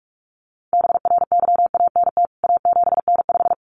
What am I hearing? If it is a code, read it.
Morse code, heard as a keyed tone, about 44 wpm, 700 Hz. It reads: BRYANT A7N5
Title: CWT Exchange 45wpm